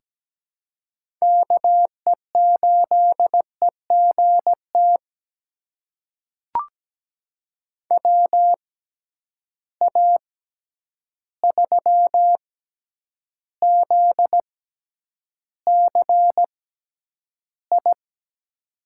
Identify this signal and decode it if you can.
Morse code, heard as a keyed tone, about 17 wpm, spaced as for 9 wpm, 700 Hz. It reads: KE8EGT WA3ZCI
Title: USA Call Signs - 6 Characters - 6x Character Spacing - 17wpm